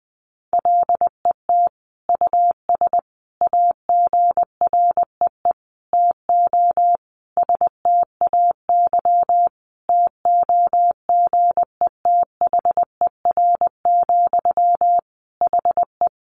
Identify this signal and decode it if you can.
Morse code, heard as a keyed tone, about 20 wpm, 700 Hz. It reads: LET US AGREE TO STAY TOGETHER, HE
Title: Sentences from Top 1100 Words 20wpm